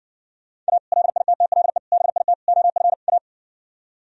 Morse code, heard as a keyed tone, about 60 wpm, 700 Hz. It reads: A LITTLE BIT OF A